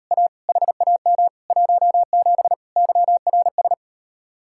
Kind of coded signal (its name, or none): Morse code